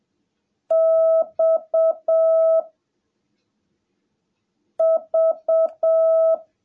Morse code, heard as a keyed tone, about 7 wpm, 650 Hz. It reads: X V
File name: jh8cbh / 2021cw / cw17a